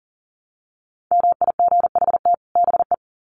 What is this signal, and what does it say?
Morse code, heard as a keyed tone, about 40 wpm, 700 Hz. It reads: MIGHT BE
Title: Sets of 2 Words 40wpm